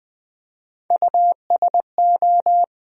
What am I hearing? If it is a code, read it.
Morse code, heard as a keyed tone, about 20 wpm, 700 Hz. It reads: USO
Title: USA Call Sign Postfixes - 3 Letters 20wpm